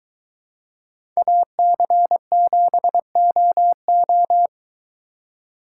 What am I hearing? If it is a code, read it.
Morse code, heard as a keyed tone, about 23 wpm, 700 Hz. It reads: AC7OO